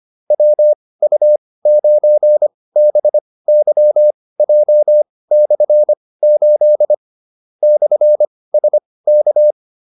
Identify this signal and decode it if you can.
Morse code, heard as a keyed tone, about 25 wpm, spaced as for 18 wpm, 600 Hz. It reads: WU9BYJ/8 /SK